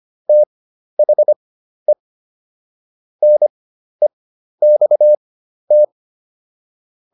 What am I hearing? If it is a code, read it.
Morse code, heard as a keyed tone, about 25 wpm, spaced as for 12 wpm, 600 Hz. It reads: THE NEXT